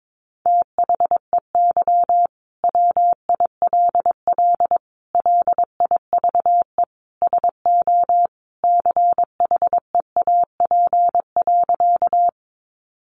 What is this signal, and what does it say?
Morse code, heard as a keyed tone, about 22 wpm, 700 Hz. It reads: THEY WILL LIVE SO CHEAP.
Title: Sentences from Top 1200 Words 22wpm